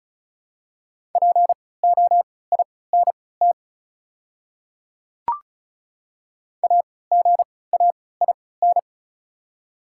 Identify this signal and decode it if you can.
Morse code, heard as a keyed tone, about 36 wpm, spaced as for 20 wpm, 700 Hz. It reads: POINT AGAIN
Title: 3x Character Spacing - Top 200 Words - 35wpm